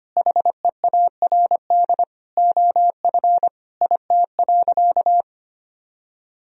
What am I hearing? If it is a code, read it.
Morse code, heard as a keyed tone, about 25 wpm, 700 Hz. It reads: HEARD OF IT.
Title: Sentences from Top 1000 Words - Review Part 3 - 25wpm